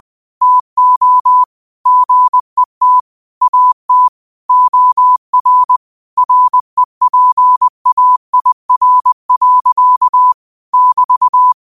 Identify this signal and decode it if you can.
Morse code, heard as a keyed tone, about 20 wpm, 1000 Hz. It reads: TO GET AT OR REPAIR. <BT>